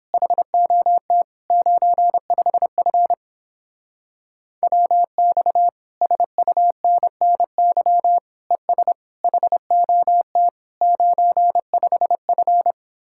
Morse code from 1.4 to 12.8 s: 95F WX SUNNY ES HOT 95F

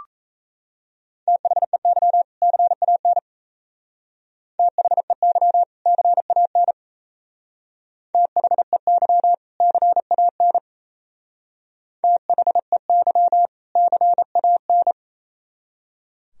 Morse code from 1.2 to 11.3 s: THEY CAN THEY CAN THEY CAN

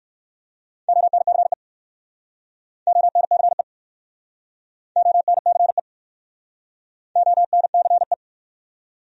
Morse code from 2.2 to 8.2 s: ONCE ONCE ONCE

